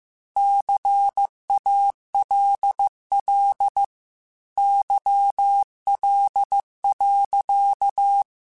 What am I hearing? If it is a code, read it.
Morse code, heard as a keyed tone, about 15 wpm, 800 Hz. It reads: CALL YL.
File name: QSOs / QSOs-mp3s / WA7QOX 2@13